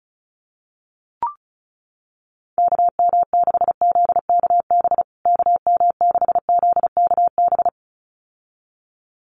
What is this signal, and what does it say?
Morse code, heard as a keyed tone, about 34 wpm, 700 Hz. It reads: KM6ZKB KM6ZKB
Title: USA Call Signs Repeated 2x 35wpm